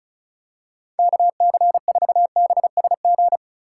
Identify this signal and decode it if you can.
Morse code, heard as a keyed tone, about 34 wpm, 700 Hz. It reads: KC4BSG
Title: USA Call Signs - 6 Characters - 4x Character Spacing - 35wpm